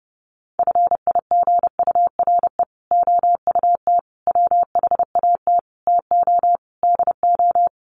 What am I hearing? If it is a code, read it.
Morse code, heard as a keyed tone, about 28 wpm, 700 Hz. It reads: FIGURE OUT WHAT TO DO
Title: Sets of 5 Words Encore2 30wpm